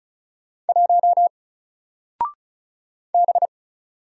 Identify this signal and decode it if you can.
Morse code, heard as a keyed tone, about 36 wpm, 700 Hz. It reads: 1 B
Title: Single Character (Letter:Number:Punctuation:Prosign) Alt 35wpm